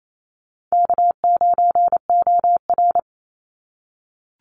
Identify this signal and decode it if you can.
Morse code, heard as a keyed tone, about 28 wpm, 700 Hz. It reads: K9OR
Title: Call Signs - 28wpm